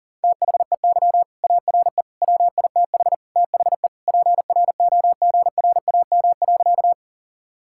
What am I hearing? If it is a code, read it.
Morse code, heard as a keyed tone, about 40 wpm, 700 Hz. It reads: THEY ARE WITH THE PROGRAM.
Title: Sentences from Top 1000 Words - Review Part 5 - 40wpm